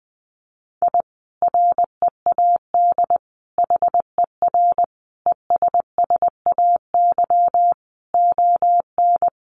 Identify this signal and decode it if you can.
Morse code, heard as a keyed tone, about 20 wpm, 700 Hz. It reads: I READ HER ESSAY ON
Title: Sentences from Top 1500 Words 20wpm